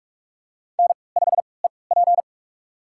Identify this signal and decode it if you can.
Morse code, heard as a keyed tone, about 46 wpm, spaced as for 25 wpm, 700 Hz. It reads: N5EP
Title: USA Call Signs - 4 Characters - 3x Character Spacing - 45wpm